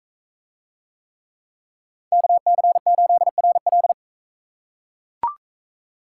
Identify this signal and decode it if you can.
Morse code, heard as a keyed tone, about 42 wpm, 700 Hz. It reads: KC8RL